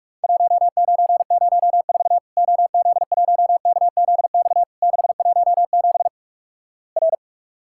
Morse code, heard as a keyed tone, about 48 wpm, 700 Hz. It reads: 1904 OZ1KZX 617 R
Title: Worked All Europe DX Contest - QTC Exchange - 45wpm